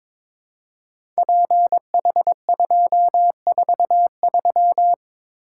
Morse code, heard as a keyed tone, about 22 wpm, 700 Hz. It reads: PH243
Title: Code Groups - 5 Letter:Numbers - 22wpm